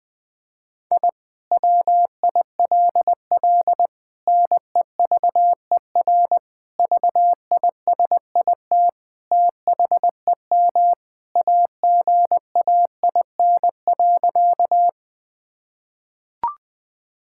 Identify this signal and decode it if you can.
Morse code, heard as a keyed tone, about 20 wpm, 700 Hz. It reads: I WILL NEVER VISIT THEM AGAIN.
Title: Sentences from Top 1000 Words - Review Part 6 - 20wpm